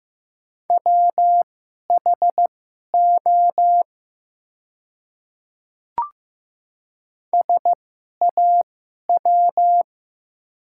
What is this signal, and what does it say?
Morse code, heard as a keyed tone, about 15 wpm, spaced as for 11 wpm, 690 Hz. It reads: WHO SAW
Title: 2x Character Spacing - Top 500 Words - 15wpm